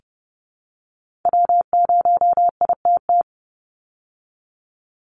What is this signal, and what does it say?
Morse code, heard as a keyed tone, about 30 wpm, 700 Hz. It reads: W0ITT
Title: Call Signs Repeated 3x - 30wpm